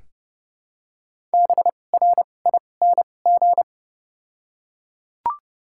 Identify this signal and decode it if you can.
Morse code, heard as a keyed tone, about 30 wpm, spaced as for 22 wpm, 700 Hz. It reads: BRING
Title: 2x Character Spacing - Top 500 Words - 30wpm